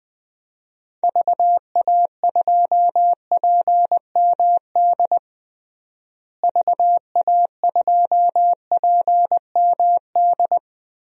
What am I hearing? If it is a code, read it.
Morse code, heard as a keyed tone, about 20 wpm, 700 Hz. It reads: VA2PMD VA2PMD
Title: DXCC-100 Call Signs - 6 Characters - Repeated 3x - 20wpm